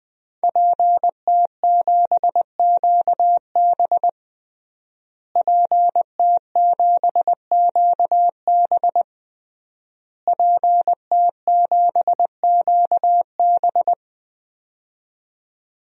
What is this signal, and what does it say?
Morse code, heard as a keyed tone, about 20 wpm, 700 Hz. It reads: PT7QB PT7QB PT7QB
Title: DXCC-100 Call Signs - 5 Characters - Repeated 3x - 20wpm